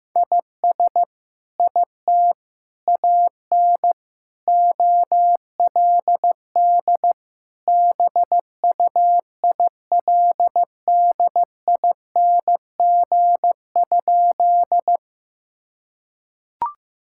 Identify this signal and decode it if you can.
Morse code, heard as a keyed tone, about 15 wpm, 700 Hz. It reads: IS IT AN OLD BUILDING?
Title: Sentences from Top 1000 Words - Review Part 5 - 15wpm